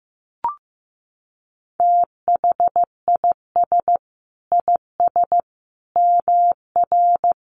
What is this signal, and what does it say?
Morse code, heard as a keyed tone, about 15 wpm, 700 Hz. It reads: THIS IS MR
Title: A Sound Of Thunder 15wpm